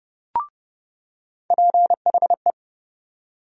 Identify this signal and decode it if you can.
Morse code, heard as a keyed tone, about 32 wpm, 700 Hz. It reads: PHE
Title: USA Call Sign Postfixes - 3 Letters 30wpm